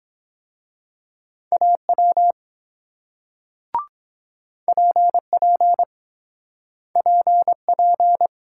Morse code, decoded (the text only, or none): AW PP PP